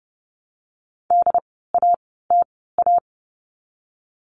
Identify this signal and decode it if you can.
Morse code, heard as a keyed tone, about 30 wpm, spaced as for 17 wpm, 700 Hz. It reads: DATA
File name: Lesson 16f - Letter c - Words 30wpm